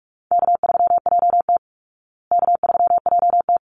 Morse code, heard as a keyed tone, about 44 wpm, 700 Hz. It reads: K3JT K3JT
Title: Easy Call Signs Repeated 2x - 45wpm